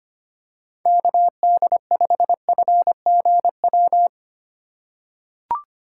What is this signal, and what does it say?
Morse code, heard as a keyed tone, about 25 wpm, 700 Hz. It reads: KD5FGW
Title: USA Call Signs - 6 Characters - 2x Character Spacing - 25wpm